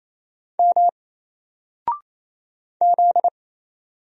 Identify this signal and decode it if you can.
Morse code, heard as a keyed tone, about 27 wpm, 700 Hz. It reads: M Z
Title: Single Letter-Number 28wpm